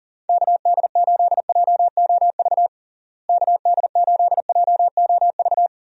Morse code, heard as a keyed tone, about 40 wpm, 700 Hz. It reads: KD8JOV KD8JOV
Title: USA Call Signs - 6 Characters - Repeated 3x - 40wpm